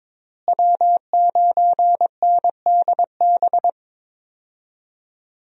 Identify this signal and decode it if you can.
Morse code, heard as a keyed tone, about 22 wpm, 700 Hz. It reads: W9NDB
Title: USA Call Signs - 5 Characters - Repeated 3x - 22wpm